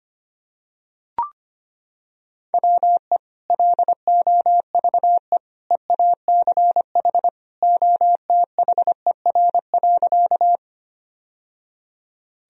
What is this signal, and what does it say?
Morse code, heard as a keyed tone, about 25 wpm, 700 Hz. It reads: WE LOVE EACH OTHER.
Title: Sentences from Top 1000 Words - Review Part 1 - 25wpm